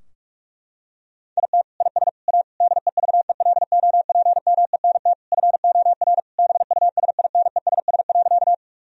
Morse code, heard as a keyed tone, about 41 wpm, 700 Hz. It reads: IT IS A DEVELOPMENT FOR BUSINESS.
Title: Sentences from Top 1000 Words - Review Part 4 - 45wpm